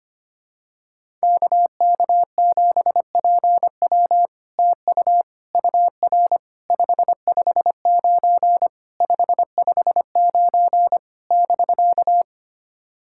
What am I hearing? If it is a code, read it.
Morse code, heard as a keyed tone, about 25 wpm, 700 Hz. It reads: KK7PW TU UR 559 559 <BK>
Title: SOTA QSO Practice 25wpm